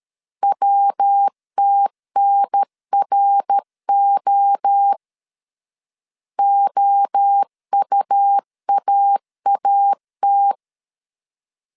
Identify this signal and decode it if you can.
Morse code, heard as a keyed tone, about 13 wpm, 800 Hz. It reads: WTNRO OUAAT
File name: CW / DET / 1 3 1500C S B 12wpm 2828